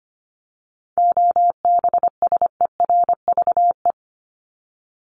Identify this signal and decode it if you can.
Morse code, heard as a keyed tone, about 25 wpm, 700 Hz. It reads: OBSERVE
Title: ED Suffix - Top 100 Words with Base Words 25wpm